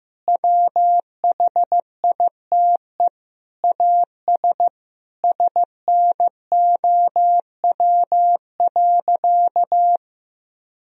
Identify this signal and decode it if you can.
Morse code, heard as a keyed tone, about 15 wpm, 700 Hz. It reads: WHITE AS SNOW.
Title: Sentences from Top 1000 Words - Review Part 3 - 15wpm